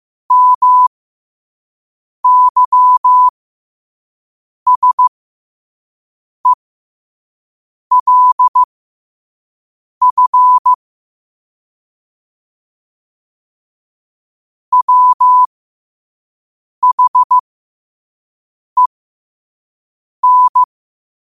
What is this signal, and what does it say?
Morse code, heard as a keyed tone, about 15 wpm, spaced as for 5 wpm, 1000 Hz. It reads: MYSELF WHEN
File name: qotdfiles / 5wpm